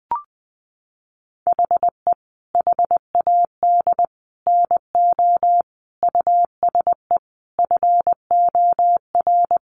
Morse code, heard as a keyed tone, about 20 wpm, 700 Hz. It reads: HE HAD NO USE FOR
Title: Sentences from Top 1200 Words 20wpm